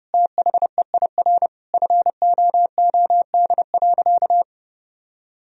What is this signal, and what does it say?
Morse code, heard as a keyed tone, about 30 wpm, 700 Hz. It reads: THEIR FOOD.
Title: Sentences from Top 1000 Words - Review Part 1 - 30wpm